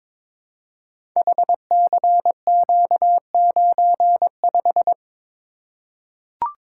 Morse code, heard as a keyed tone, about 22 wpm, 700 Hz. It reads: HCQ95
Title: Code Groups - 5 Letter:Numbers - 22wpm